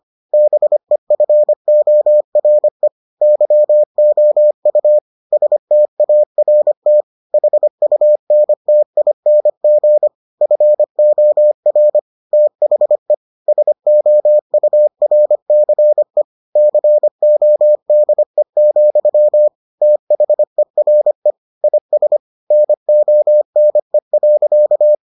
Morse code, heard as a keyed tone, about 25 wpm, 600 Hz. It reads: BEFORE YOU START HUNTING FOR THE SOURCE CODE, THERE IS NONE.